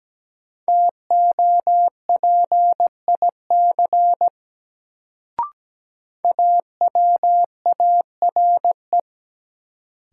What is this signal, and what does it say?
Morse code, heard as a keyed tone, about 17 wpm, 700 Hz. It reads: TOPIC AWARE
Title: Sentences from Top 1200 Words - New Word Drill Alt - 17wpm